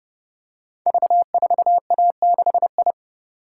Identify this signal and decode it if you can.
Morse code, heard as a keyed tone, about 29 wpm, 700 Hz. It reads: V4A6I
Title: Code Groups - 5 Letter:Numbers - 30wpm